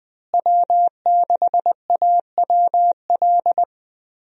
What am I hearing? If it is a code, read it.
Morse code, heard as a keyed tone, about 20 wpm, 700 Hz. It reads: W6AWL
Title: USA Call Signs - 5 Characters - Repeated 2x - 20wpm